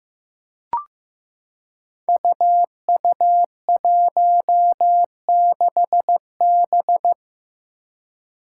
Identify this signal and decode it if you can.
Morse code, heard as a keyed tone, about 15 wpm, 700 Hz. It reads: UU16B
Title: Five Challenging Letter-Number Combinations 15wpm